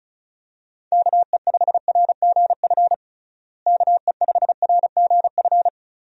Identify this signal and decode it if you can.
Morse code, heard as a keyed tone, about 36 wpm, 700 Hz. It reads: KE5RGF KE5RGF
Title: USA Call Signs - 6 Characters - Repeated 2x - 35wpm